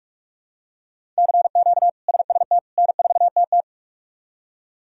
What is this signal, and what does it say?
Morse code, heard as a keyed tone, about 46 wpm, 700 Hz. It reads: CQ SST N4TT
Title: K1USN SST QSO Practice - 45wpm